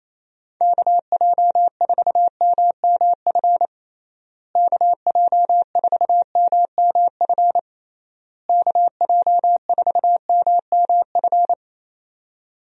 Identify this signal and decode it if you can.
Morse code, heard as a keyed tone, about 27 wpm, 700 Hz. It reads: KJ4MMF KJ4MMF KJ4MMF
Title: USA Call Signs - 6 Characters - Repeated 3x - 28wpm